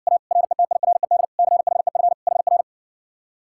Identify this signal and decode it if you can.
Morse code, heard as a keyed tone, about 60 wpm, 700 Hz. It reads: A RETIRED CHEF HR